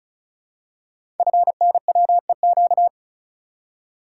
Morse code, heard as a keyed tone, about 36 wpm, 700 Hz. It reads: FNWEQ
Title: Code Groups - 5 Letters - 35wpm